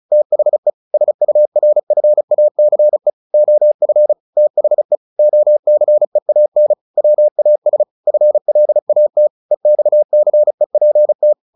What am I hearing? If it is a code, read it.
Morse code, heard as a keyed tone, about 34 wpm, 600 Hz. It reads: THE SURFACE OF THE OCEAN WAS FLAT EXCEPT
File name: the old man and the sea / the old man and the sea 1 35wpm